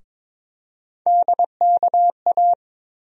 Morse code, heard as a keyed tone, about 22 wpm, 700 Hz. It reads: DKA
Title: USA Call Sign Postfixes - 3 Letters 22wpm